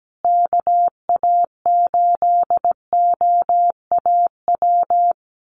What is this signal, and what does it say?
Morse code, heard as a keyed tone, about 17 wpm, 700 Hz. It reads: KA8OAW
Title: USA Call Signs Repeated 3x 17wpm